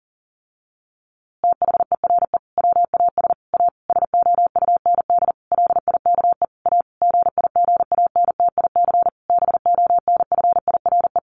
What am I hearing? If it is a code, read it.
Morse code, heard as a keyed tone, about 40 wpm, 700 Hz. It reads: THERE WAS A SOUND LIKE A GIGANTIC BONFIRE
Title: A Sound Of Thunder 40wpm